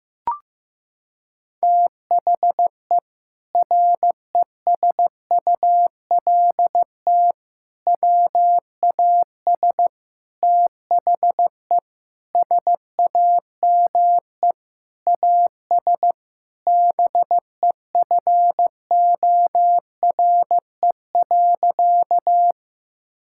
Morse code, heard as a keyed tone, about 15 wpm, 700 Hz. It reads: THE RESULT WAS THE SAME AS BEFORE.
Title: Sentences from Top 1000 Words - Review Part 5 - 15wpm